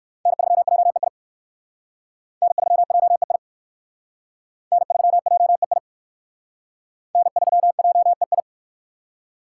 Morse code, heard as a keyed tone, about 61 wpm, 700 Hz. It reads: N3JEI N3JEI N3JEI N3JEI